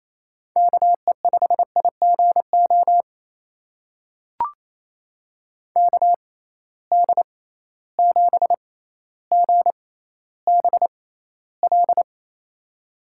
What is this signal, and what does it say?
Morse code, heard as a keyed tone, about 28 wpm, spaced as for 14 wpm, 700 Hz. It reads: KE5IGO KD7GBL
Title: USA Call Signs - 6 Characters - 6x Character Spacing - 28wpm